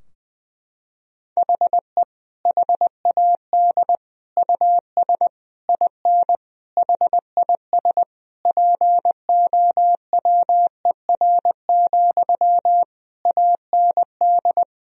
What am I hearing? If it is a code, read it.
Morse code, heard as a keyed tone, about 20 wpm, 700 Hz. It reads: HE HAD US IN HIS POWER, AND
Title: Sentences from Top 1000 Words - Review Part 2 - 20wpm